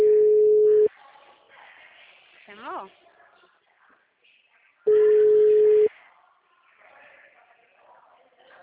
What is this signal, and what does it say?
Ringback tone (Europe, 425 Hz), with speech.